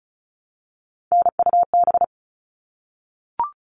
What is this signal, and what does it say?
Morse code, heard as a keyed tone, about 34 wpm, 700 Hz. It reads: NUB
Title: All Cut Numbers - 3-Digit 35wpm